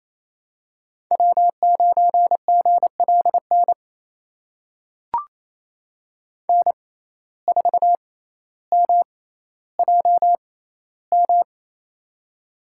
Morse code, heard as a keyed tone, about 28 wpm, spaced as for 14 wpm, 700 Hz. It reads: W9GLN N4MJM
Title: USA Call Signs - 5 Characters - 6x Character Spacing - 28wpm